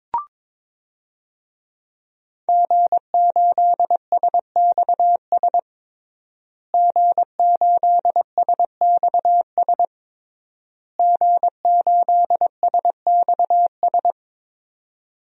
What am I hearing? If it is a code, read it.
Morse code, heard as a keyed tone, about 22 wpm, 700 Hz. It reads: G8SXS G8SXS G8SXS
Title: DXCC-100 Call Signs - 5 Characters - Repeated 3x - 22wpm